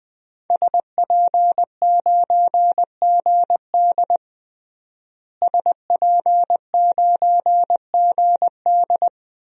Morse code, heard as a keyed tone, about 20 wpm, 700 Hz. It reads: SP9GD SP9GD
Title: DXCC-100 Call Signs - 5 Characters - Repeated 3x - 20wpm